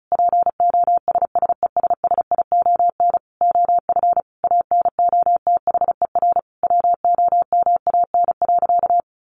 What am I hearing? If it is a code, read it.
Morse code, heard as a keyed tone, about 34 wpm, 700 Hz. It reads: POSSESSION OF ANOTHER WOMAN.
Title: Sentences from Top 1400 Words 35wpm